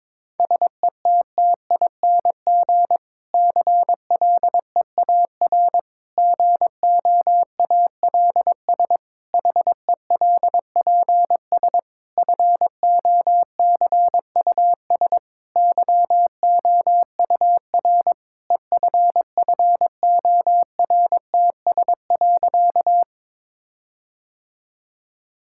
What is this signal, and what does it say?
Morse code, heard as a keyed tone, about 22 wpm, 700 Hz. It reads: SETTING CLEAR GOALS HELPS FOCUS YOUR EFFORTS.
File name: Difficult Sentences with Hints - 22wpm